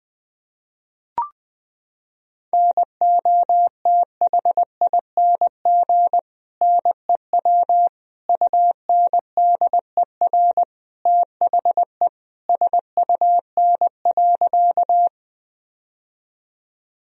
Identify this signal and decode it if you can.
Morse code, heard as a keyed tone, about 20 wpm, 700 Hz. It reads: NOTHING NEW UNDER THE SUN.
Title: Sentences from Top 1000 Words - Review Part 1 - 20wpm